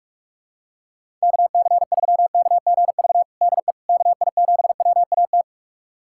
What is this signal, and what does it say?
Morse code, heard as a keyed tone, about 41 wpm, 700 Hz. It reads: KC3KGV DE KI7WAT
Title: USA Callsign DE Callsign 45wpm